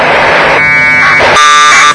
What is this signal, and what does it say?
Packet radio (AX.25, as in APRS).
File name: vk3ukf / ISS packet QSO